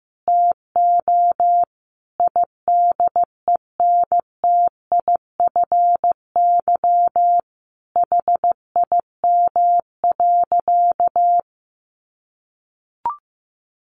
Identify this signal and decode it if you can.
Morse code, heard as a keyed tone, about 15 wpm, 700 Hz. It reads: TO IDENTIFY HIM.
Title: Sentences from Top 900 Words 15wpm